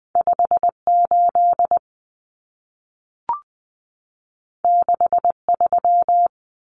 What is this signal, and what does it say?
Morse code, heard as a keyed tone, about 20 wpm, 700 Hz. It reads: 58 63
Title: Common Cut Numbers - 2-Digit 20wpm